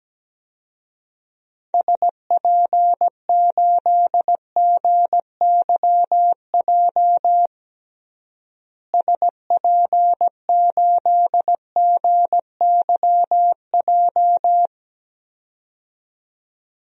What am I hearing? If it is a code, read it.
Morse code, heard as a keyed tone, about 17 wpm, 700 Hz. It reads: SP8GYJ SP8GYJ
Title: DXCC-100 Call Signs - 6 Characters - Repeated 2x - 17wpm